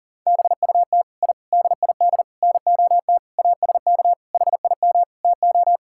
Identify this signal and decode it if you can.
Morse code, heard as a keyed tone, about 40 wpm, 700 Hz. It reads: BUT I DID NOT ASK HIM TO
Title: Sentences from Top 1000 Words - Review Part 5 - 40wpm